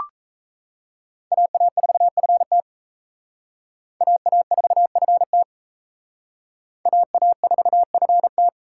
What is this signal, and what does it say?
Morse code, heard as a keyed tone, about 38 wpm, 700 Hz. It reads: AA4FT AA4FT AA4FT